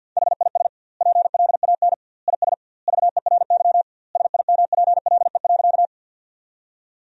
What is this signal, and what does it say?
Morse code, heard as a keyed tone, about 46 wpm, 700 Hz. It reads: HIS PLAN IS VERY SIMPLE.
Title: 2x Word-Spacing - Sentences from Top 500 Words - 50wpm